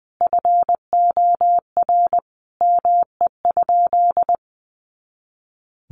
Morse code, heard as a keyed tone, about 20 wpm, 700 Hz. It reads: FOR ME?
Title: Sentences from Top 500 Words 20wpm